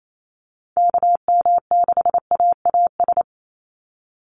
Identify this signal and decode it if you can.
Morse code, heard as a keyed tone, about 28 wpm, 700 Hz. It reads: KM6AAS
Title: USA Call Signs 28wpm